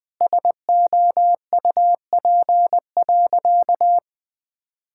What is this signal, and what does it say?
Morse code, heard as a keyed tone, about 21 wpm, 700 Hz. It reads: SOUP.
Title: Sentences from Top 1000 Words - Review Part 6 - 20wpm